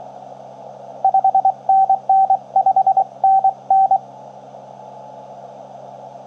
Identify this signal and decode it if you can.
Morse code, heard as a keyed tone, about 26 wpm, 750 Hz. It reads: 5NN 5NN
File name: k9w-wake-atoll-on-10m-cw-via-long-path-in-jn90ab